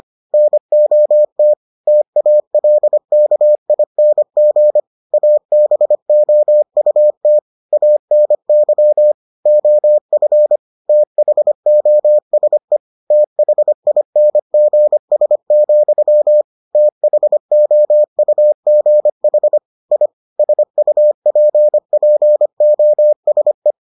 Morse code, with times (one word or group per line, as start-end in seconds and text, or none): 0.3-1.5 NOT
1.9-4.8 TALKING
5.1-7.4 ABOUT
7.7-9.1 ANY
9.5-10.6 OF
10.9-12.8 THOSE
13.1-16.4 THINGS,
16.8-19.6 THOUGH
19.9-20.1 I
20.4-23.7 SUPPOSE